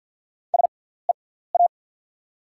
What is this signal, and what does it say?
Morse code, heard as a keyed tone, about 48 wpm, spaced as for 17 wpm, 690 Hz. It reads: SEA
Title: 6x Character Spacing - Top 500 Words - 50wpm